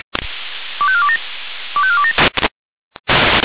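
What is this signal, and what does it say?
Five-tone selective call (ZVEI), heard as quick runs of five tones, tones of about 70 ms. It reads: 25427 25427
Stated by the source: ZVEI five-tone call 25417 received on an SDR by the Orlamünde fire brigade (feuerwehr-orlamuende).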